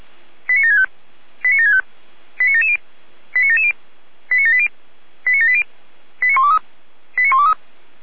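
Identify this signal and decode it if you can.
Five-tone selective call (ZVEI), heard as quick runs of five tones, tones of about 70 ms. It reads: 89765 89765 89770 89770 89780 89780 89123 89123
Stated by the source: ZVEI five-tone call received on SDR.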